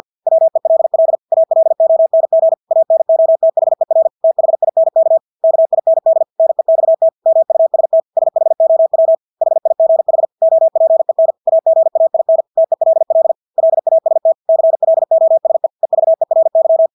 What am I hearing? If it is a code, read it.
Morse code, heard as a keyed tone, about 50 wpm, 650 Hz. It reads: WELL ALONG ANOTHER THINK KIND NEXT MUST SHOW HIGH OPEN AGAIN TELL LAST CLOSE EVERY
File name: Engelske ord 50 wpm